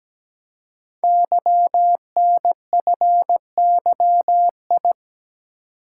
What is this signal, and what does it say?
Morse code, heard as a keyed tone, about 17 wpm, 700 Hz. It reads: YNFYI
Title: Code Groups - 5 Letters - 17wpm